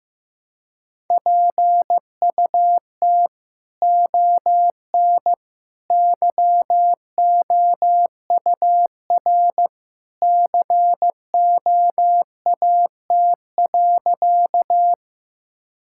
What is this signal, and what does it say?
Morse code, heard as a keyed tone, about 15 wpm, 700 Hz. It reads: PUT ON YOUR COAT.
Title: Sentences from Top 1600 Words 15wpm